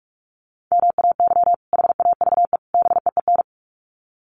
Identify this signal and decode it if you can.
Morse code, heard as a keyed tone, about 44 wpm, 700 Hz. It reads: MAY HAVE BEEN
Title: Sets of 3 Words 45wpm